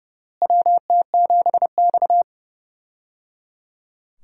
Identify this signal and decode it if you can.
Morse code, heard as a keyed tone, about 30 wpm, 700 Hz. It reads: WT7X
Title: USA Call Signs - 4 Characters - Repeated 3x - 30wpm